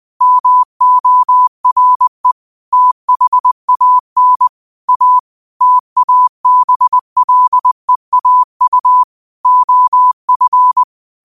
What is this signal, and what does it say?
Morse code, heard as a keyed tone, about 20 wpm, 1000 Hz. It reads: MORE THAN A TABLEAU OF
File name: qotdfiles / 20wpm